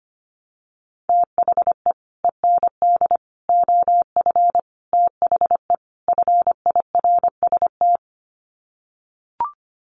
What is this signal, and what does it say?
Morse code, heard as a keyed tone, about 25 wpm, 700 Hz. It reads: THE END OF THE FIRST
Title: Sets of 5 Words Encore 25wpm